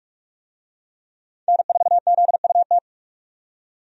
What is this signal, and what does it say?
Morse code, heard as a keyed tone, about 47 wpm, 700 Hz. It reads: N4ZUT
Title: USA Call Signs - 5 Characters - 45wpm